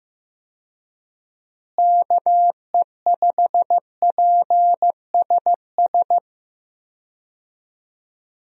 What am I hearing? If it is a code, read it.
Morse code, heard as a keyed tone, about 16 wpm, 700 Hz. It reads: KE5PSS
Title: USA Call Signs - 6 Characters - Repeated 3x - 15wpm